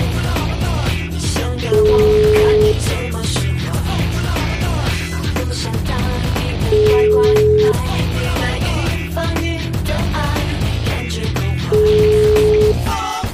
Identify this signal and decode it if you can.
Ringback tone (Europe, 425 Hz), with noise.